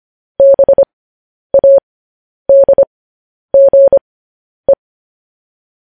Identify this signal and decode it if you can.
Morse code, heard as a keyed tone, about 25 wpm, spaced as for 10 wpm, 550 Hz. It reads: BADGE
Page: news headlines in morse code at 10 wpm